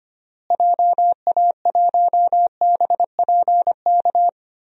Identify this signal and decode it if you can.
Morse code, heard as a keyed tone, about 25 wpm, 700 Hz. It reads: JA1BPK
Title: DXCC-100 Call Signs - 6 Characters - Repeated 3x - 25wpm